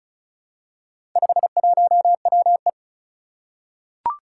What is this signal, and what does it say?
Morse code, heard as a keyed tone, about 36 wpm, 700 Hz. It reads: 51WE
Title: Four Challenging Letter-Number Combinations 35wpm